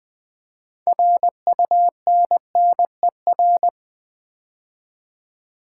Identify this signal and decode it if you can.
Morse code, heard as a keyed tone, about 20 wpm, 700 Hz. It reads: RUNNER
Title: Build A Word - 2 Letters At A Time - 6-Letter Words - 20wpm